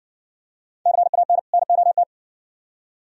Morse code, heard as a keyed tone, about 57 wpm, 700 Hz. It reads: CAN NOT